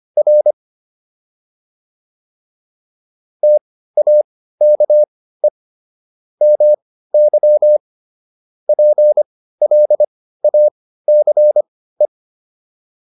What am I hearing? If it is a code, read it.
Morse code, heard as a keyed tone, about 25 wpm, spaced as for 15 wpm, 600 Hz. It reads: R TAKE MY PLACE